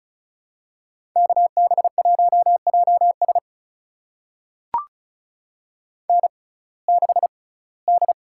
Morse code, heard as a keyed tone, about 36 wpm, spaced as for 21 wpm, 700 Hz. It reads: KB1JS N6D
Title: USA Call Signs - 5 Characters - 6x Character Spacing - 35wpm